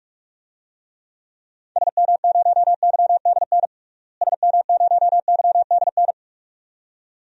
Morse code, heard as a keyed tone, about 43 wpm, 700 Hz. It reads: SM0YDN SM0YDN
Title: DXCC-100 Call Signs - 6 Characters - Repeated 2x - 45wpm